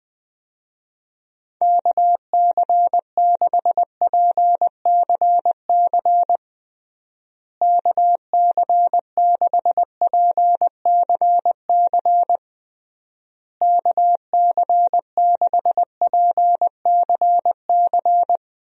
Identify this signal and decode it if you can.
Morse code, heard as a keyed tone, about 21 wpm, 700 Hz. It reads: KC6PCC KC6PCC KC6PCC
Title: USA Call Signs - 6 Characters - Repeated 3x - 20wpm